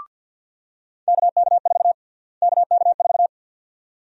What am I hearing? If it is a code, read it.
Morse code, heard as a keyed tone, about 46 wpm, 700 Hz. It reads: KK4 KK4
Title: USA Call Sign Prefixes - Repeated 2x - 50wpm